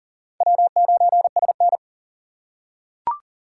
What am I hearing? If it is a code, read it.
Morse code, heard as a keyed tone, about 40 wpm, 700 Hz. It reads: W9SN
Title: USA Call Signs - 4 Characters - 2x Character Spacing - 40wpm